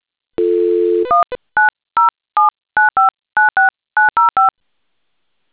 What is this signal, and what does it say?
DTMF, heard as telephone touch tones, tones of about 150 ms. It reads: 19079596905